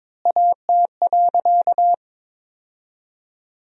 Morse code, heard as a keyed tone, about 23 wpm, 700 Hz. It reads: AT.
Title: 4x Word-Spacing - Sentences from Top 200 Words - 22wpm